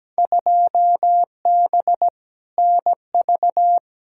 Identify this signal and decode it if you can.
Morse code, heard as a keyed tone, about 17 wpm, 700 Hz. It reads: 2B NV
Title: Field Day Exchanges Encore - 17wpm